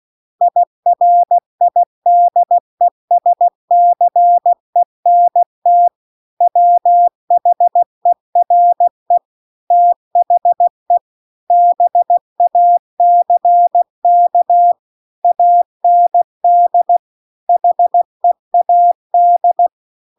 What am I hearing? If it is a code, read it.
Morse code, heard as a keyed tone, about 16 wpm, 700 Hz. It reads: IRIDESCENT WHERE THE BACK AND HEAD